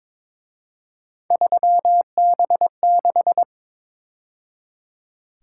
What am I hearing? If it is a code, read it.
Morse code, heard as a keyed tone, about 26 wpm, 700 Hz. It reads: 3B6
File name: Speed-Racing - Three Letter-Number Combinations - 22wpm